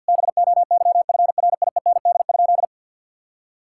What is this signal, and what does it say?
Morse code, heard as a keyed tone, about 50 wpm, 700 Hz. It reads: BOYFRIEND?
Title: Sentences from Top 1000 Words - Review Part 5 - 50wpm